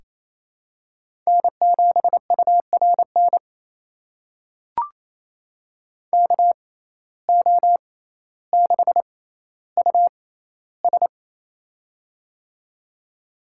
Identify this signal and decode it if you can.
Morse code, heard as a keyed tone, about 28 wpm, spaced as for 14 wpm, 700 Hz. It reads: N7URN KO6US
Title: USA Call Signs - 6x Character Spacing - 28wpm